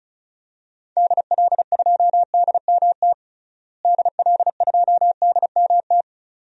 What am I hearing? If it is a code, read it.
Morse code, heard as a keyed tone, about 36 wpm, 700 Hz. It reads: DL2DMT DL2DMT
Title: DXCC-100 Call Signs - 6 Characters - Repeated 2x - 35wpm